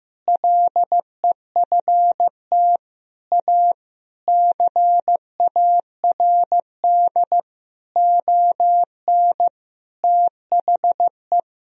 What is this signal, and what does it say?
Morse code, heard as a keyed tone, about 16 wpm, 700 Hz. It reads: LEFT A CARD ON THE
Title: Sentences from Top 1000 Words - Review Part 4 - 15wpm